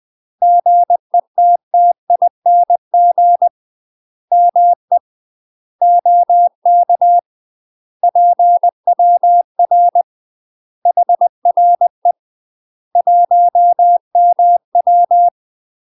Morse code, heard as a keyed tone, about 20 wpm, 700 Hz. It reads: GETTING ME OK PWR HRE 1MW
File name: UK MASTER QSOs TRAINING / 20WPM / UK MASTER QSOs 0001